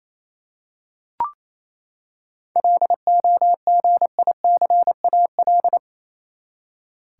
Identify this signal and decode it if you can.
Morse code, heard as a keyed tone, about 28 wpm, 700 Hz. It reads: LOGICAL
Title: Seven Letter Words - 28wpm